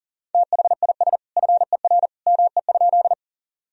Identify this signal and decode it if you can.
Morse code, heard as a keyed tone, about 40 wpm, 700 Hz. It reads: THIS FER ME?
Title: Sentences from Top 200 Words and Abbreviations 40wpm